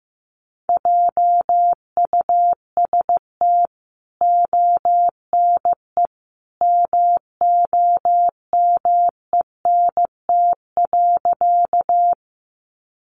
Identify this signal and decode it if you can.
Morse code, heard as a keyed tone, about 15 wpm, 700 Hz. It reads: JUST ONE MOMENT.
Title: Sentences from Top 1000 Words 15wpm